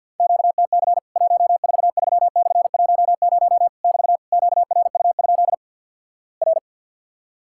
Morse code, heard as a keyed tone, about 50 wpm, 700 Hz. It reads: QTC 143/10 <BT> QRU? R
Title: Worked All Europe DX Contest - QTC Exchange with Repeats - 50wpm